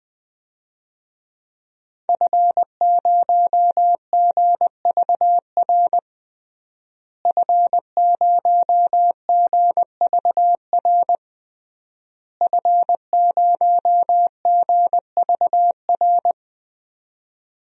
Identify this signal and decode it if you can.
Morse code, heard as a keyed tone, about 21 wpm, 700 Hz. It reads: F0GVR F0GVR F0GVR
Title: DXCC-100 Call Signs - 5 Characters - Repeated 3x - 20wpm